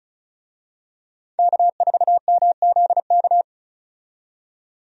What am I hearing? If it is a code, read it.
Morse code, heard as a keyed tone, about 36 wpm, 700 Hz. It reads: K4MZK
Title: USA Call Signs - 5 Characters - 35wpm